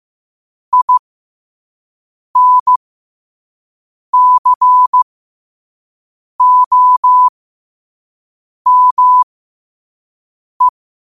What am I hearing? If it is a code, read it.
Morse code, heard as a keyed tone, about 15 wpm, spaced as for 5 wpm, 1000 Hz. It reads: INCOME
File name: qotdfiles / 5wpm